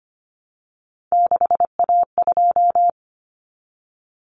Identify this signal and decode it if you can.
Morse code, heard as a keyed tone, about 25 wpm, 700 Hz. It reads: 6A2
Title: Common Cut Numbers - 3-Digit 25wpm